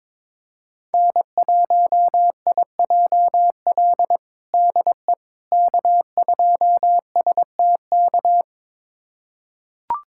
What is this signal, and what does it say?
Morse code, heard as a keyed tone, about 22 wpm, 700 Hz. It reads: N1IJL DE K2STK
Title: USA Callsign DE Callsign - 5 Characters - 22wpm